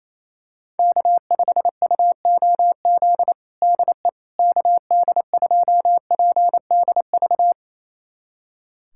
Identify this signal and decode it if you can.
Morse code, heard as a keyed tone, about 28 wpm, 700 Hz. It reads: K5UOZ DE KD2PDV
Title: USA Callsign DE Callsign 28wpm